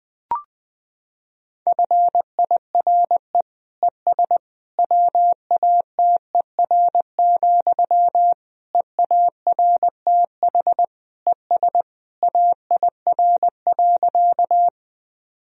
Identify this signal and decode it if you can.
Morse code, heard as a keyed tone, about 21 wpm, 700 Hz. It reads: FIRE ES WATER, EARTH ES AIR.
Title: Sentences from Top 400 Words and Abbreviations 20wpm